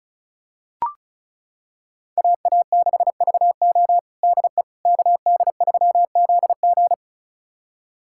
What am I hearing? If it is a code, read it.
Morse code, heard as a keyed tone, about 34 wpm, 700 Hz. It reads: AA6VO DE KD3ZG
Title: USA Callsign DE Callsign - 5 Characters - 35wpm